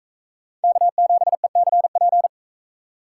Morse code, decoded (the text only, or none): K7ECP